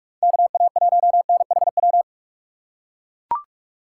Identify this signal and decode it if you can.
Morse code, heard as a keyed tone, about 43 wpm, 700 Hz. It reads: KA1NHW